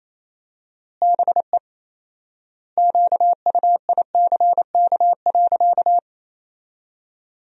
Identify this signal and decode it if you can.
Morse code, heard as a keyed tone, about 28 wpm, 700 Hz. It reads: BE QUICK.
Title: 4x Word-Spacing - Sentences from Top 500 Words - 28wpm